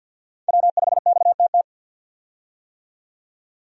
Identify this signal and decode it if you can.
Morse code, heard as a keyed tone, about 52 wpm, 700 Hz. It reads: W5XTT
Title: USA Call Signs - 5 Characters - 50wpm